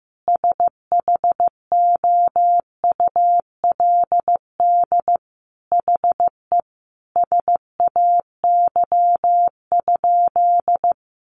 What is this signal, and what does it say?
Morse code, heard as a keyed tone, about 15 wpm, 700 Hz. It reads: SHOULD HE SAY?
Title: Sentences from Top 300 Words 15wpm